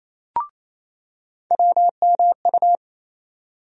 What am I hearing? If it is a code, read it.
Morse code, heard as a keyed tone, about 28 wpm, 700 Hz. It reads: WMU
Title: USA Call Sign Postfixes - 3 Letters 28wpm